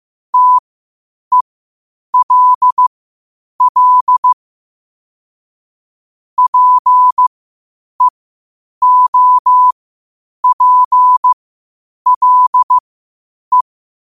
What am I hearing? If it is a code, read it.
Morse code, heard as a keyed tone, about 15 wpm, spaced as for 8 wpm, 1000 Hz. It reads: TELL PEOPLE
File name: qotdfiles / 8wpm